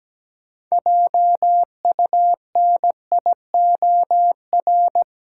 Morse code, heard as a keyed tone, about 17 wpm, 700 Hz. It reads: JUNIOR
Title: Sentences from Top 1500 Words - New Word Drill Alt - 17wpm